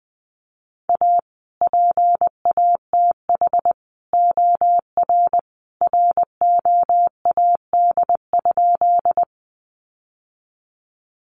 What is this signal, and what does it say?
Morse code, heard as a keyed tone, about 19 wpm, 700 Hz. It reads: A PATH OR ROAD?
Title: Sentences from Top 1400 Words 20wpm